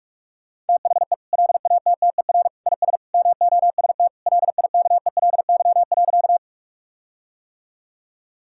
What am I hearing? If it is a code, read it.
Morse code, heard as a keyed tone, about 41 wpm, 700 Hz. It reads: THE LATTER IS MOST LIKELY.
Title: Sentences from Top 1600 Words 45wpm